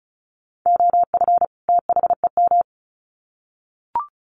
Morse code, heard as a keyed tone, about 34 wpm, 700 Hz. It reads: OF THEM
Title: Sets of 2 Words 35wpm